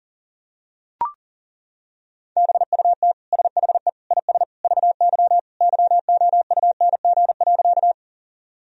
Morse code, heard as a keyed tone, about 40 wpm, 700 Hz. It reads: BUT SHE IS VY YOUNG.
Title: Sentences from Top 400 Words and Abbreviations 40wpm